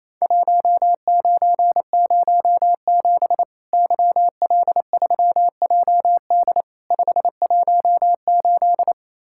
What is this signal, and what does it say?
Morse code, heard as a keyed tone, about 28 wpm, 700 Hz. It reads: 1907 YL3JD 518
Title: Worked All Europe DX Contest - QTC Exchange - 28wpm